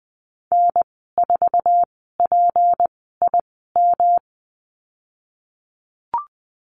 Morse code, heard as a keyed tone, about 20 wpm, spaced as for 15 wpm, 700 Hz. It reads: N4PIM
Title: 161 Lesson 26g - Letter g - Call Signs 20wpm